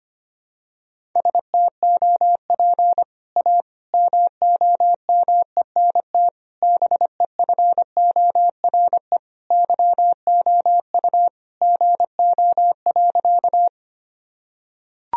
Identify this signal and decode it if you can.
Morse code, heard as a keyed tone, about 25 wpm, 700 Hz. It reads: STOP A MOMENT BEFORE YOU GO.
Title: Sentences from Top 1000 Words - Review Part 1 - 25wpm